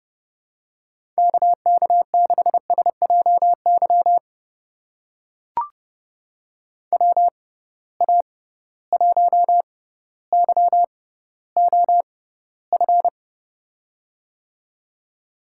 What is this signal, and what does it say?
Morse code, heard as a keyed tone, about 30 wpm, spaced as for 15 wpm, 700 Hz. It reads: KK6SJY WA1YOF
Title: USA Call Signs - 6 Characters - 6x Character Spacing - 30wpm